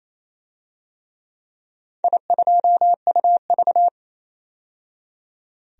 Morse code, heard as a keyed tone, about 27 wpm, 700 Hz. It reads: I2UV
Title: DXCC-100 Call Signs - 4 Characters - 28wpm